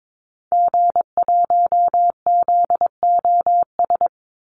Morse code, heard as a keyed tone, about 22 wpm, 700 Hz. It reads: G1ZOS
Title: Call Signs - 22wpm